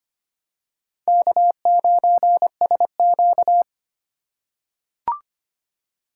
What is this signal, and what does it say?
Morse code, heard as a keyed tone, about 25 wpm, 700 Hz. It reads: K9SQ
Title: USA Call Signs - 4 Characters - 2x Character Spacing - 25wpm